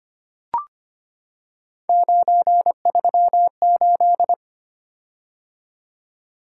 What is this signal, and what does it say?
Morse code, heard as a keyed tone, about 25 wpm, 700 Hz. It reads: 938
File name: Three Letter-Number Combinations - 25wpm